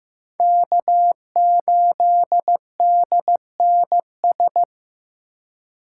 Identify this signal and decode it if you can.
Morse code, heard as a keyed tone, about 18 wpm, 700 Hz. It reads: K8DNS
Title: Speed Racing - USA 1x3 Call Signs 15wpm